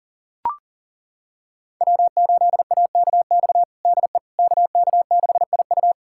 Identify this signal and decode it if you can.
Morse code, heard as a keyed tone, about 37 wpm, 700 Hz. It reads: W8AKX DE KK6IU
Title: USA Callsign DE Callsign - 5 Characters - 40wpm